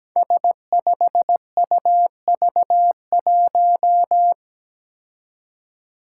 Morse code, heard as a keyed tone, about 17 wpm, 700 Hz. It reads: S5UV1
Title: Five Challenging Letter-Number Combinations 17wpm